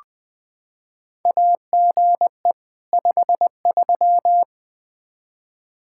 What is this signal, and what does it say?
Morse code, heard as a keyed tone, about 20 wpm, 700 Hz. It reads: AGE 53
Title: K1USN SST QSO Practice 20wpm